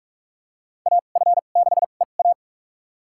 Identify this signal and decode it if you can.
Morse code, heard as a keyed tone, about 46 wpm, spaced as for 32 wpm, 700 Hz. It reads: AF6EA
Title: USA Call Signs - 2x Character Spacing - 45wpm